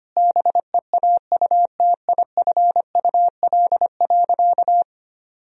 Morse code, heard as a keyed tone, about 25 wpm, 700 Hz. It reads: BEAUTIFUL.
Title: Sentences from Top 1000 Words - Review Part 4 - 25wpm